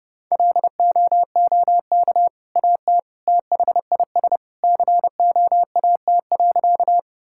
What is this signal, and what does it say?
Morse code, heard as a keyed tone, about 30 wpm, 700 Hz. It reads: LOOK AT THIS COAT.
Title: Sentences from Top 1600 Words 30wpm